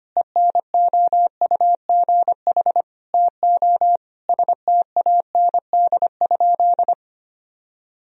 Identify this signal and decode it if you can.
Morse code, heard as a keyed tone, about 25 wpm, 700 Hz. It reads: ENOUGH TO STAND?
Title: Sentences from Top 1000 Words - Review Part 3 - 25wpm